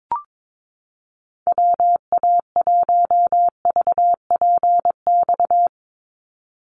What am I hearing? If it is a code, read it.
Morse code, heard as a keyed tone, about 22 wpm, 700 Hz. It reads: WA1VPX
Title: USA Call Signs 22wpm